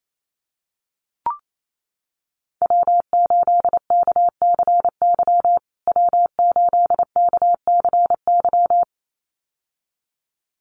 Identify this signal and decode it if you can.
Morse code, heard as a keyed tone, about 28 wpm, 700 Hz. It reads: W8KCY W8KCY
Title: USA Call Signs Repeated 2x 28wpm